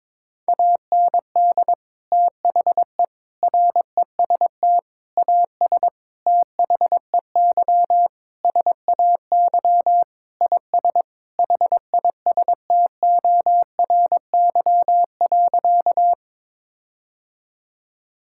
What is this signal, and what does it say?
Morse code, heard as a keyed tone, about 22 wpm, 700 Hz. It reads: AND THE REST AS THEY SAY IS HISTORY.
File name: practice / 22wpm / Sentences with 8-Word Phrases - 22wpm